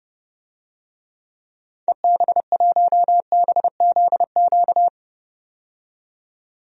Morse code, heard as a keyed tone, about 30 wpm, 700 Hz. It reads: EB1BZQ
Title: DXCC-100 Call Signs - 6 Characters - 30wpm